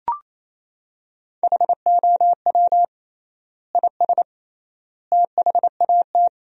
Morse code, heard as a keyed tone, about 28 wpm, 700 Hz. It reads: HOW IS THAT
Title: 3x Word-Spacing - Sentences from Top 500 Words - 28wpm